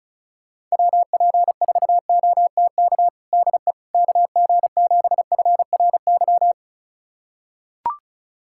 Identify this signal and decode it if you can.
Morse code, heard as a keyed tone, about 36 wpm, 690 Hz. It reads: WP4OTK DE KG7FRY
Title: USA Callsign DE Callsign - 6 Characters - 35wpm